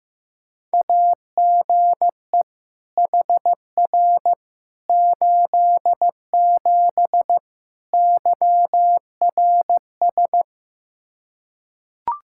Morse code, heard as a keyed tone, about 16 wpm, 700 Hz. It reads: AGE HR 87 YRS
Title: Practice QSO - Exchange 3 - More Info - 15wpm